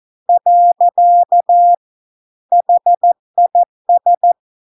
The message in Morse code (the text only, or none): . HIS